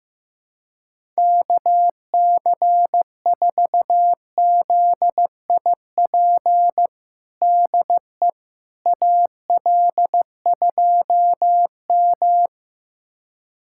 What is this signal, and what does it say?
Morse code, heard as a keyed tone, about 15 wpm, 700 Hz. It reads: KC4ZIP DE AL2M
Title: USA Callsign DE Callsign 15wpm